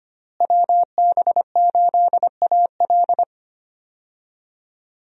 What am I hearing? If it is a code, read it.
Morse code, heard as a keyed tone, about 25 wpm, 700 Hz. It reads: WB8AL
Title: USA Call Signs - 5 Characters - Repeated 2x - 25wpm